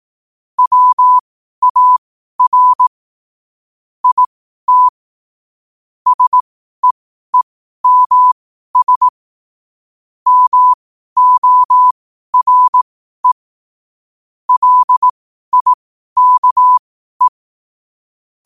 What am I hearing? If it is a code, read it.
Morse code, heard as a keyed tone, about 18 wpm, spaced as for 12 wpm, 1000 Hz. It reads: WAR IT SEEMS MORE LIKE
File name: 12wpm